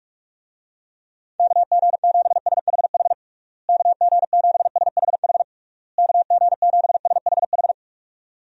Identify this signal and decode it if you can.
Morse code, heard as a keyed tone, about 46 wpm, 700 Hz. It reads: KG7SHH KG7SHH KG7SHH
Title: USA Call Signs - 6 Characters - Repeated 3x - 45wpm